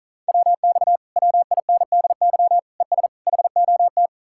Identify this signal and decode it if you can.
Morse code, heard as a keyed tone, about 44 wpm, 700 Hz. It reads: WX WINDY ES HOT